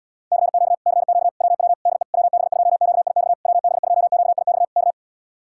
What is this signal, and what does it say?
Morse code, heard as a keyed tone, about 80 wpm, 700 Hz. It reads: CQ CQ WW DE WB2QIX WB2QIX K